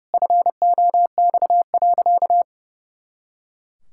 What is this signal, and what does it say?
Morse code, heard as a keyed tone, about 30 wpm, 700 Hz. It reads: FOX.
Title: Pangram Challenge v2 - 30wpm